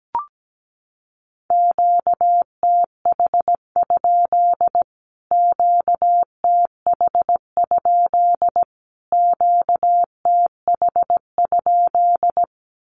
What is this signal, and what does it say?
Morse code, heard as a keyed tone, about 17 wpm, 700 Hz. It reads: QTH? QTH? QTH?
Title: Q-Codes Repeated 3x 17wpm